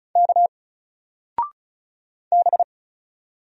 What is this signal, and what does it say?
Morse code, heard as a keyed tone, about 36 wpm, 700 Hz. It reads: K B
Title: Single Character (Letter:Number:Punctuation:Prosign) 35wpm